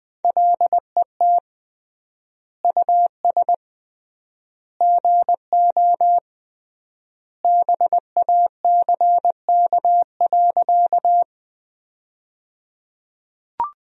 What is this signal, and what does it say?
Morse code, heard as a keyed tone, about 20 wpm, 700 Hz. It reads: LET US GO BACK.
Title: 3x Word-Spacing - Sentences from Top 300 Words - 20wpm